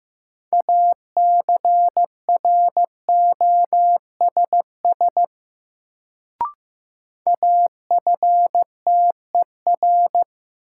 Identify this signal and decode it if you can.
Morse code, heard as a keyed tone, about 15 wpm, 700 Hz. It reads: ACROSS AFTER
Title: Sets of 2 Words - New Word Drill - 15wpm